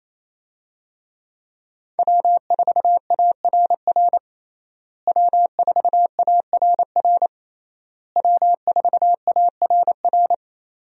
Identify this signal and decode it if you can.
Morse code, heard as a keyed tone, about 28 wpm, 700 Hz. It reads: W4ARR W4ARR W4ARR
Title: USA Call Signs - 5 Characters - Repeated 3x - 28wpm